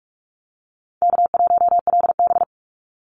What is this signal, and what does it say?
Morse code, heard as a keyed tone, about 44 wpm, 700 Hz. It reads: K1LB
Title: NA SOTA Activator Callsigns 45wpm